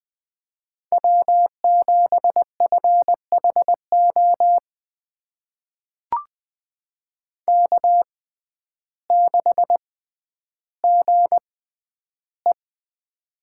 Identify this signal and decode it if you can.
Morse code, heard as a keyed tone, about 20 wpm, spaced as for 11 wpm, 700 Hz. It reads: W7FHO K6GE
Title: USA Call Signs - 6x Character Spacing - 20wpm